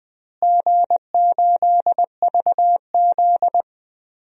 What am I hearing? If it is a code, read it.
Morse code, heard as a keyed tone, about 21 wpm, 700 Hz. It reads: G8VZ
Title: DXCC-100 Call Signs - 4 Characters - Repeated 3x - 20wpm